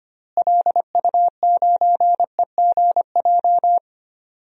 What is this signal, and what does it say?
Morse code, heard as a keyed tone, about 25 wpm, 700 Hz. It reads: LU9EGJ
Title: DXCC-100 Call Signs - 6 Characters - Repeated 2x - 25wpm